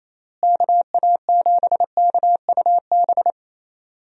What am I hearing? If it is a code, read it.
Morse code, heard as a keyed tone, about 28 wpm, 700 Hz. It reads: KA7KUB
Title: USA Call Signs - 6 Characters - Repeated 2x - 28wpm